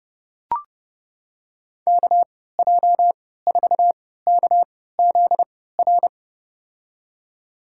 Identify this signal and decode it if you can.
Morse code, heard as a keyed tone, about 30 wpm, spaced as for 17 wpm, 700 Hz. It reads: KJ4KZR
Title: USA Call Signs - 6 Characters - 3x Character Spacing - 30wpm